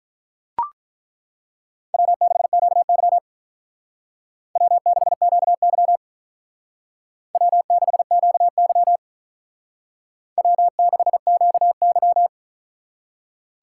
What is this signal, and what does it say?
Morse code, heard as a keyed tone, about 42 wpm, 700 Hz. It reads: W6QY W6QY W6QY W6QY
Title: Speed Racing - USA 1x2 Call Signs 35wpm